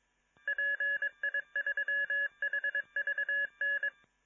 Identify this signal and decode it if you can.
Morse code, heard as a keyed tone, about 22 wpm, 1700 Hz. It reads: PI3HVN